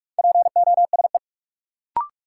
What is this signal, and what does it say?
Morse code, heard as a keyed tone, about 46 wpm, 700 Hz. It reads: POSE
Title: Sentences from Top 1000 Words - Review Part 2 - New Word Drill - 45wpm